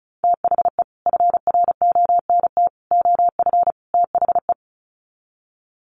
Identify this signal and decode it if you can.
Morse code, heard as a keyed tone, about 34 wpm, 700 Hz. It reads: THE FRONT OF THE
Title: Sets of 4 Words 35wpm